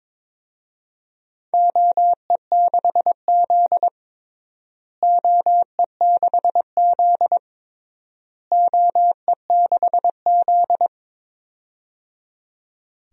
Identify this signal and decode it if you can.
Morse code, heard as a keyed tone, about 22 wpm, 700 Hz. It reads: OE6Z OE6Z OE6Z
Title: DXCC-100 Call Signs - 4 Characters - Repeated 3x - 22wpm